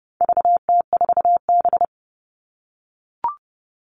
Morse code, heard as a keyed tone, about 30 wpm, 700 Hz. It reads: VT4B
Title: All Cut Numbers - 4-Digit 30wpm